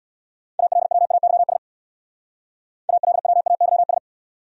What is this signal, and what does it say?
Morse code, heard as a keyed tone, about 59 wpm, 700 Hz. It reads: ALWAYS ALWAYS
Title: Speed-Racing Top 500 Words 50wpm